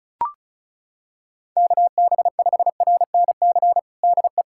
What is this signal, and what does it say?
Morse code, heard as a keyed tone, about 36 wpm, 700 Hz. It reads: KB5RNC DE
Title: USA Callsign DE Callsign - 6 Characters - 35wpm